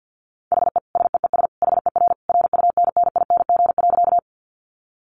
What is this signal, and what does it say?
Morse code, heard as a keyed tone, about 50 wpm, 700 Hz. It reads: HE SEES HER RUNNING.
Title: Additional Easy Phrases 50wpm